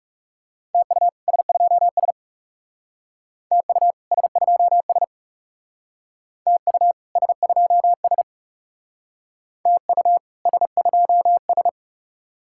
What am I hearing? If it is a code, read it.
Morse code, heard as a keyed tone, about 39 wpm, 700 Hz. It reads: TU S2S TU S2S TU S2S TU S2S